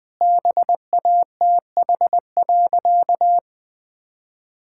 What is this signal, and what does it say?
Morse code, heard as a keyed tone, about 20 wpm, 700 Hz. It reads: BATH.
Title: Easy Sentences with Hints - 20wpm